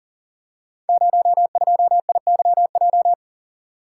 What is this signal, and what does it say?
Morse code, heard as a keyed tone, about 42 wpm, 700 Hz. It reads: 02IYJ
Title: Code Groups - 5 Letter:Numbers - 40wpm